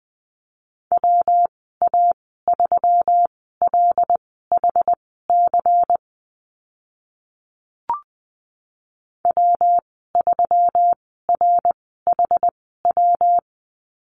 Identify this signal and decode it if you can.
Morse code, heard as a keyed tone, about 20 wpm, spaced as for 14 wpm, 700 Hz. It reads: WA3LHC W3RHW
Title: 140 Lesson 23g - Letter f - Call Signs 20wpm